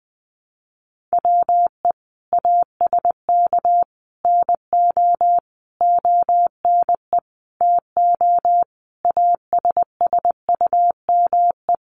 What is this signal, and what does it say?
Morse code, heard as a keyed tone, about 20 wpm, 700 Hz. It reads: WE ASK NO ONE TO ASSUME
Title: Sentences from Top 1500 Words 20wpm